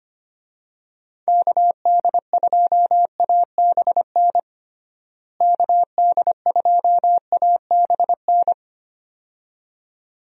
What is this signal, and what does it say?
Morse code, heard as a keyed tone, about 25 wpm, 700 Hz. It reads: KD2ABN KD2ABN
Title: USA Call Signs - 6 Characters - Repeated 2x - 25wpm